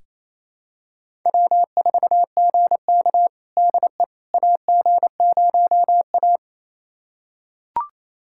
Morse code, heard as a keyed tone, about 27 wpm, 700 Hz. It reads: W4GK DE AG0A
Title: USA Callsign DE Callsign - 4 Characters - 28wpm